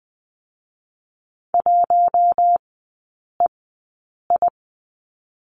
Morse code, heard as a keyed tone, about 19 wpm, spaced as for 8 wpm, 700 Hz. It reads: 1EI
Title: 041 Lesson 8e - Number 1 - Three Characters 20wpm